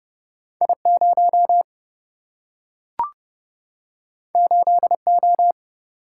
Morse code, heard as a keyed tone, about 30 wpm, 700 Hz. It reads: I0 8O
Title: Single-Letter Single-Number 30wpm